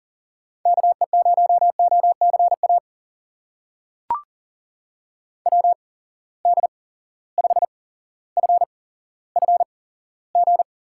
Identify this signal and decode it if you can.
Morse code, heard as a keyed tone, about 40 wpm, spaced as for 17 wpm, 700 Hz. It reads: KE0OCA WD5FFG
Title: USA Call Signs - 6 Characters - 8x Character Spacing - 40wpm